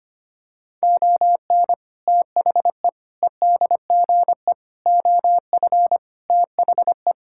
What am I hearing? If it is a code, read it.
Morse code, heard as a keyed tone, about 25 wpm, 700 Hz. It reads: ON THE EDGE OF THE
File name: practice / 25wpm / Sentences with 3-Word Phrases - 25wpm